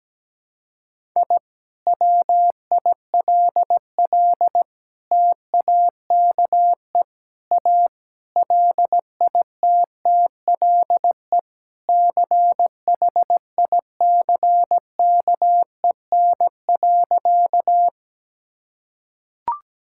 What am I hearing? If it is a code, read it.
Morse code, heard as a keyed tone, about 17 wpm, 700 Hz. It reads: I WILL TAKE A LITTLE CHICKEN.
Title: Sentences from Top 1000 Words - Review Part 5 - 17wpm